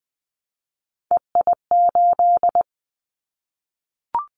Morse code, heard as a keyed tone, about 20 wpm, 700 Hz. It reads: EI8
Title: Callsign Prefixes for Top 50 DXCC - 20wpm